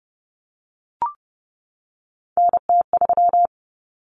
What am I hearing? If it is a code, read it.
Morse code, heard as a keyed tone, about 30 wpm, 700 Hz. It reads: NT3
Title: Common Cut Numbers - 3-Digit 30wpm